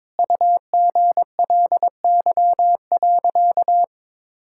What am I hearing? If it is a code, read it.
Morse code, heard as a keyed tone, about 22 wpm, 700 Hz. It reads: UGLY.
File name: Sentences with 7-Word Phrases - 22wpm